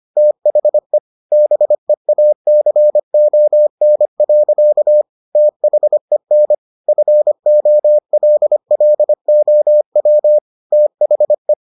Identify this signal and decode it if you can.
Morse code, heard as a keyed tone, about 25 wpm, 600 Hz. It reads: THE BEACON. THEN FOLLOW THE